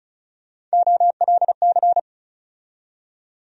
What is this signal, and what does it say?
Morse code, heard as a keyed tone, about 36 wpm, 700 Hz. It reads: OLC
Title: USA Call Sign Postfixes - 3 Letters 35wpm